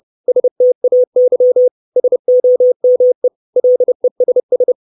Morse code, heard as a keyed tone, about 30 wpm, 500 Hz. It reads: STAY SOME LESS
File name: 632 Words 30WPM